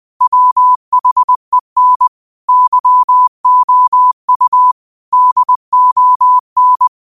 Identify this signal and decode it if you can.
Morse code, heard as a keyed tone, about 20 wpm, 1000 Hz. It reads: WHEN YOU DON